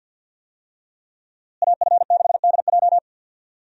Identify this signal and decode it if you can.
Morse code, heard as a keyed tone, about 52 wpm, 700 Hz. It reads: AF6DJ
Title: USA Call Signs - 5 Characters - Repeated 3x - 50wpm